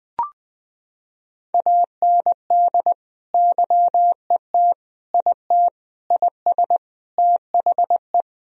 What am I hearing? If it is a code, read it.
Morse code, heard as a keyed tone, about 20 wpm, 700 Hz. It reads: AND YET IT IS THE
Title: Sentences from Top 1000 Words - Review Part 3 - 20wpm